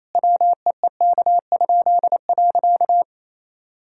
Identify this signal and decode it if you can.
Morse code, heard as a keyed tone, about 27 wpm, 700 Hz. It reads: WEEK?.
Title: Pangram Challenge v2 - 28wpm